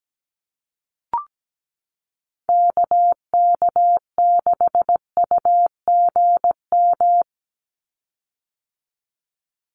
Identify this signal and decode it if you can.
Morse code, heard as a keyed tone, about 17 wpm, 700 Hz. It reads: KK6UGM
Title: USA Call Signs 17wpm